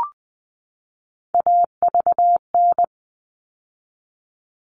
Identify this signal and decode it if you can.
Morse code, heard as a keyed tone, about 20 wpm, 700 Hz. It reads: AVN
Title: All Cut Numbers - 3-Digit 20wpm